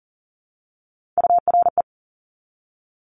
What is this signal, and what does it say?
Morse code, heard as a keyed tone, about 38 wpm, 700 Hz. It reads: URE
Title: Top Three Letter Combinations in Words 40wpm